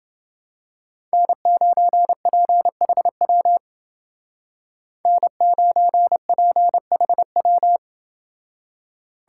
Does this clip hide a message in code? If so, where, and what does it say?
Yes: Morse code from 1.1 to 7.8 s, heard as a keyed tone, about 28 wpm, 700 Hz. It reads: N9PHW N9PHW